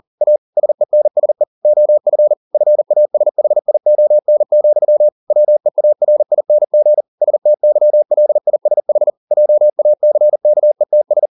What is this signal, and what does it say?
Morse code, heard as a keyed tone, about 40 wpm, 600 Hz. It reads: A SENSE OF FASHION, WEARING STYLISH JACKETS